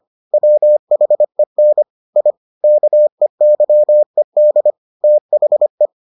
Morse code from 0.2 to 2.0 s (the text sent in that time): WHEN